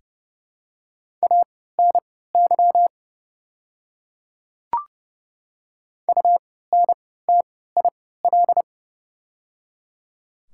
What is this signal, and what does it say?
Morse code, heard as a keyed tone, about 30 wpm, spaced as for 17 wpm, 700 Hz. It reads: ANY UNTIL
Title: 3x Character Spacing - Top 300 Words - 30wpm